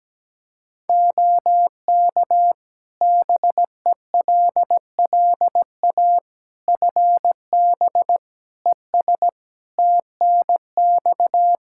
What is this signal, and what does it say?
Morse code, heard as a keyed tone, about 17 wpm, 700 Hz. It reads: OK BELLA FB ES TNX
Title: K1USN SST QSO Practice 17wpm